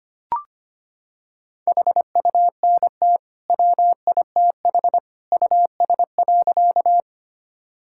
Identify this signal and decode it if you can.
Morse code, heard as a keyed tone, about 25 wpm, 700 Hz. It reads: HUNT WITH US.
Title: Sentences from Top 1600 Words 25wpm